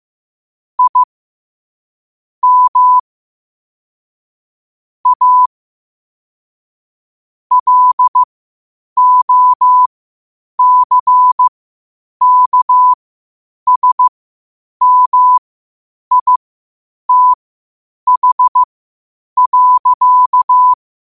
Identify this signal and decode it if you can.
Morse code, heard as a keyed tone, about 15 wpm, spaced as for 8 wpm, 1000 Hz. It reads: I M A LOCKSMITH.
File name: qotdfiles / 8wpm